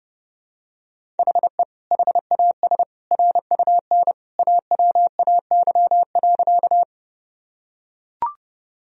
Morse code, heard as a keyed tone, about 30 wpm, 700 Hz. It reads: HE HAS RUN AWAY.
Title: Sentences from Top 1000 Words - Review Part 1 - 30wpm